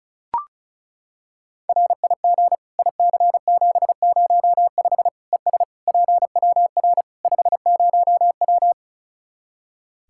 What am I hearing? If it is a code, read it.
Morse code, heard as a keyed tone, about 34 wpm, 700 Hz. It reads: RIG IC705 ES PWR 50W
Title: K1USN SST QSO Practice 35wpm